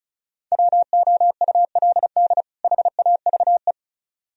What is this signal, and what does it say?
Morse code, heard as a keyed tone, about 46 wpm, 700 Hz. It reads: WOULD HAVE